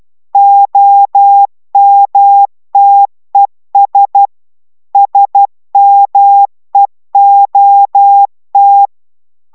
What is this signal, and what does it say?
Morse code, heard as a keyed tone, about 12 wpm, 800 Hz. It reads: OMTES SMEOT